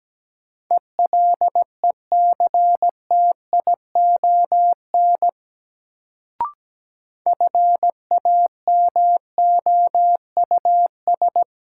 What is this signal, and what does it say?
Morse code, heard as a keyed tone, about 17 wpm, 700 Hz. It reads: ELECTION FAMOUS
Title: Sentences from Top 1500 Words - New Word Drill - 17wpm